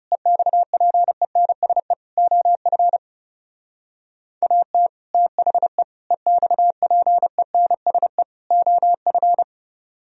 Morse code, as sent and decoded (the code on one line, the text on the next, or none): . -..- .--. -. ... / --- ..-. / .- - / - .... / . -..- .--. -. ... / --- ..-.
EXPENSE OF AT THE EXPENSE OF